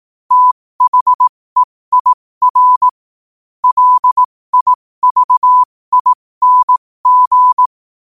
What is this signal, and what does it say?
Morse code, heard as a keyed tone, about 18 wpm, spaced as for 15 wpm, 1000 Hz. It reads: THEIR LIVING
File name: 15wpm